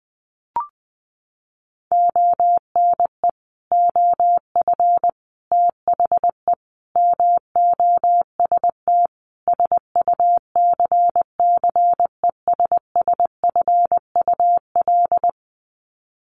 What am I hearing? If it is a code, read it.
Morse code, heard as a keyed tone, about 20 wpm, 700 Hz. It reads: ONE OF THE MOST SUCCESSFUL
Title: Sets of 5 Words Encore2 20wpm